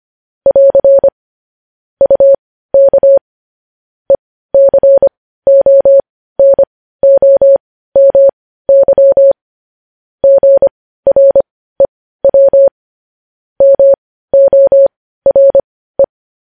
Morse code, heard as a keyed tone, about 25 wpm, spaced as for 15 wpm, 550 Hz. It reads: <AR> UK ECONOMY GREW MORE
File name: podcast / News headlines-1x-25x15